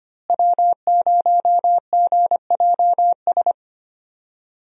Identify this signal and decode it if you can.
Morse code, heard as a keyed tone, about 25 wpm, 700 Hz. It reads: W0GJS
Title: USA Call Signs - 5 Characters - Repeated 3x - 25wpm